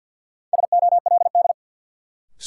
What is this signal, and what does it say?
Morse code, heard as a keyed tone, about 53 wpm, 700 Hz. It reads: SOLD
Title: Sentences from Top 1400 Words - New Word Drill Alt - 50wpm